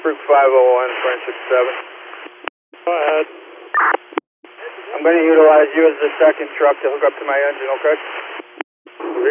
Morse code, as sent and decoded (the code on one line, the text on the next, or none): none